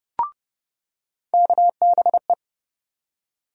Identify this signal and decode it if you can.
Morse code, heard as a keyed tone, about 30 wpm, 700 Hz. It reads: KBE
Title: USA Call Sign Postfixes - 3 Letters 30wpm